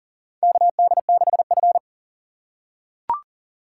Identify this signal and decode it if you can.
Morse code, heard as a keyed tone, about 38 wpm, 700 Hz. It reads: KD6F
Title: USA Call Signs - 4 Characters - 2x Character Spacing - 40wpm